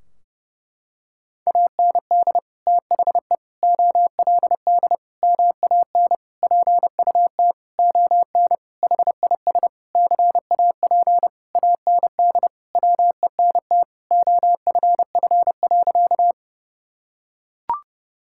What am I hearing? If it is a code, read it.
Morse code, heard as a keyed tone, about 30 wpm, 700 Hz. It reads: AND THE OLD MAN PUT ON HIS CAP AND WENT OFF.
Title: Sentences from Top 1600 Words 30wpm